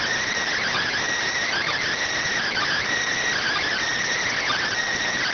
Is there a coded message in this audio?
Yes: Emergency Alert System (SAME) tones from 0.5 to 4.7 s, heard as harsh buzzing bursts.